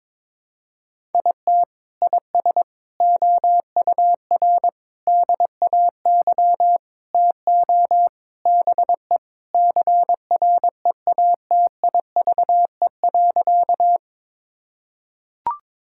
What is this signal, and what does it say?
Morse code, heard as a keyed tone, about 22 wpm, 700 Hz. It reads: IT IS OUR DAY TO BE CREATIVE.
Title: Sentences from Top 1000 Words - Review Part 5 - 22wpm